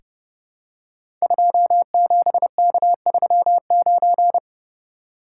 Morse code, heard as a keyed tone, about 32 wpm, 690 Hz. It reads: 27K39
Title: Code Groups - 5 Letter:Numbers - 30wpm